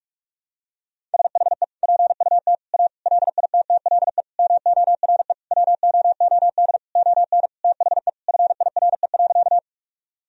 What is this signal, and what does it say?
Morse code, heard as a keyed tone, about 46 wpm, 700 Hz. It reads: SHE PUT A LITTLE MORE WOOD ON THE FIRE.
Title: Sentences from Top 1000 Words - Review Part 2 - 45wpm